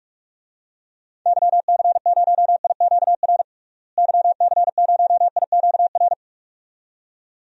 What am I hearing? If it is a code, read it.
Morse code, heard as a keyed tone, about 46 wpm, 700 Hz. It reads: YC0IQR YC0IQR
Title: DXCC-100 Call Signs - 6 Characters - Repeated 2x - 45wpm